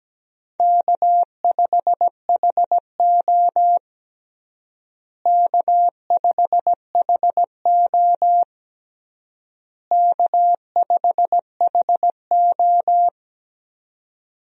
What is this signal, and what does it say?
Morse code, heard as a keyed tone, about 17 wpm, 700 Hz. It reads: K5HO K5HO K5HO
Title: USA Call Signs - 4 Characters - Repeated 3x - 17wpm